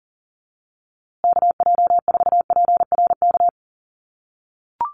Morse code, heard as a keyed tone, about 40 wpm, 700 Hz. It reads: KJ4PRK
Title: USA Call Signs Repeated 3x 40wpm